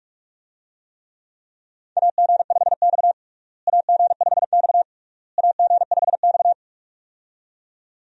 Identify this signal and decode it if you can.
Morse code, heard as a keyed tone, about 42 wpm, 700 Hz. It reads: AG5X AG5X AG5X
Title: USA Call Signs - 4 Characters - Repeated 3x - 45wpm